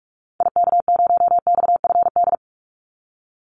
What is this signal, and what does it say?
Morse code, heard as a keyed tone, about 46 wpm, 700 Hz. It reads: IK0XFD
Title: Call Signs - 45wpm